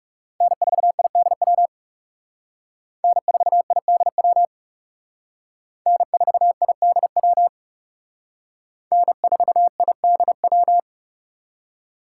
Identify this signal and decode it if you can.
Morse code, heard as a keyed tone, about 37 wpm, 700 Hz. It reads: N4IDW N4IDW N4IDW N4IDW
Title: Speed Racing - USA 1x3 Call Signs 30wpm